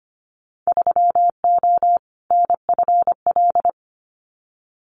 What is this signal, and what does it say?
Morse code, heard as a keyed tone, about 25 wpm, 700 Hz. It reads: 3O NFL
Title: Winter Field Day Exchanges 25wpm